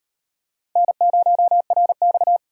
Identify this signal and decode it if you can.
Morse code, heard as a keyed tone, about 34 wpm, 700 Hz. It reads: N0RX